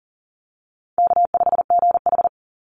Morse code, heard as a keyed tone, about 40 wpm, 700 Hz. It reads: K5GH
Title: Easy Call Signs Repeated 2x - 40wpm